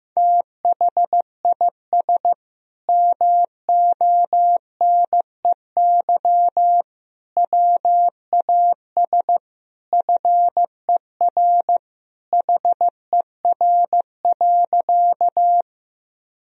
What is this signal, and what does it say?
Morse code, heard as a keyed tone, about 15 wpm, 700 Hz. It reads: THIS MONEY WAS FER HER.
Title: Sentences from Top 400 Words and Abbreviations 15wpm